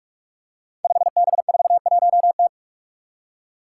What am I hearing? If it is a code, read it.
Morse code, heard as a keyed tone, about 41 wpm, 700 Hz. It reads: 5B41T
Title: Five Challenging Letter-Number Combinations 45wpm